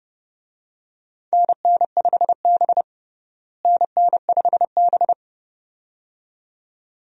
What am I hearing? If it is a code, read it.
Morse code, heard as a keyed tone, about 30 wpm, 700 Hz. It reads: NN5B NN5B
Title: USA Call Signs - 4 Characters - Repeated 2x - 30wpm